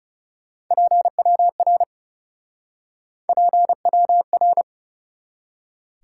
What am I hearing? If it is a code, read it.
Morse code, heard as a keyed tone, about 39 wpm, 700 Hz. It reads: PWR PWR